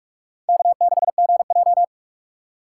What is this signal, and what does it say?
Morse code, heard as a keyed tone, about 46 wpm, 700 Hz. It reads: K6GJ
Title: USA Call Signs - 4 Characters - 3x Character Spacing - 45wpm